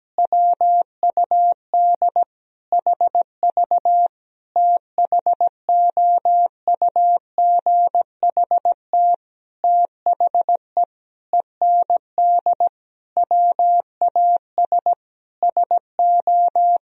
Morse code, heard as a keyed tone, about 17 wpm, 700 Hz. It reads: WUD HV THOUGHT THE END WAS SO
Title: Sentences from Top 300 Words and Abbreviations 17wpm